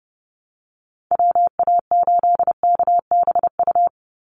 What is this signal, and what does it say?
Morse code, heard as a keyed tone, about 30 wpm, 700 Hz. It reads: WA8KBU
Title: Call Signs Repeated 3x - 30wpm